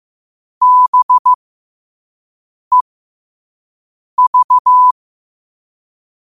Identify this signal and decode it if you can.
Morse code, heard as a keyed tone, about 15 wpm, spaced as for 5 wpm, 1000 Hz. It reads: BEV
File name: 5wpm